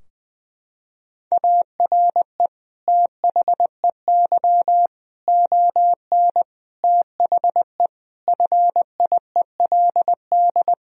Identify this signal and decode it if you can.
Morse code, heard as a keyed tone, about 19 wpm, 700 Hz. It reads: ARE THEY ON THE FIELD
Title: Sentences from Top 1000 Words - Review Part 2 - 20wpm